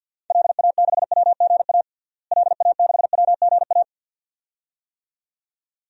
Morse code, heard as a keyed tone, about 49 wpm, 700 Hz. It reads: LA6WGA LA6WGA
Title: DXCC-100 Call Signs - 6 Characters - Repeated 2x - 50wpm